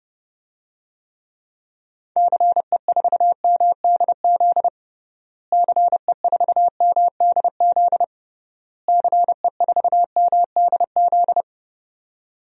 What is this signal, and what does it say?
Morse code, heard as a keyed tone, about 32 wpm, 700 Hz. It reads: CE4MDZ CE4MDZ CE4MDZ
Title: DXCC-100 Call Signs - 6 Characters - Repeated 3x - 30wpm